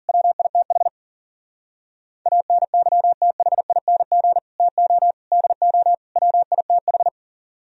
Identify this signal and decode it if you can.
Morse code, heard as a keyed tone, about 51 wpm, 700 Hz. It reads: WITH ANYTHING TO DO WITH